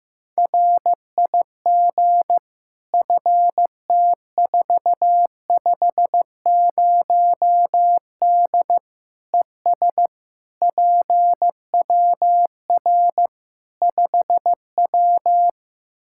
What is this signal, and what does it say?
Morse code, heard as a keyed tone, about 15 wpm, 700 Hz. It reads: RIG FT450D ES PWR 5W
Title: Practice QSO - Exchange 2 - Info - 15wpm